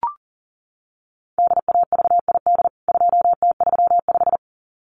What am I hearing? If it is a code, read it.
Morse code, heard as a keyed tone, about 40 wpm, 700 Hz. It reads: DAVID 2T35
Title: CWT Exchange 40wpm